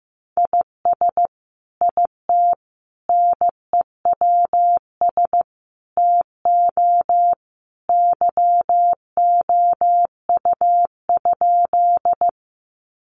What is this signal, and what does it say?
Morse code, heard as a keyed tone, about 15 wpm, 700 Hz. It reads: IS IT NEWS TO YOU?
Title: Sentences from Top 800 Words 15wpm